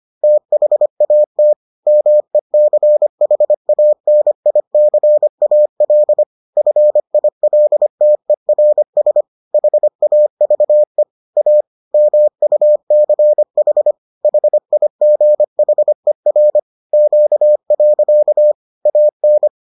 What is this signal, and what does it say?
Morse code, heard as a keyed tone, about 25 wpm, 600 Hz. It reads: THAT MECHANICAL FILTERS HAVE A MUCH HIGHER Q. AN